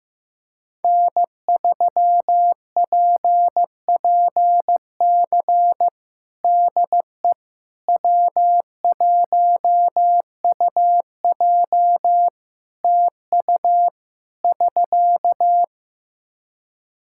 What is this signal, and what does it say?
Morse code, heard as a keyed tone, about 15 wpm, 700 Hz. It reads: N3PPC DE W1UJ TU <SK>
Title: Practice QSO - Exchange 4 - Ending - 15wpm